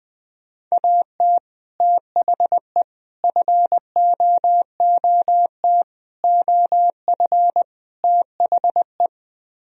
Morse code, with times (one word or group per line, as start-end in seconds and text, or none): 0.7-1.4 AT
1.8-2.8 THE
3.2-5.8 FOOT
6.2-7.6 OF
8.0-9.1 THE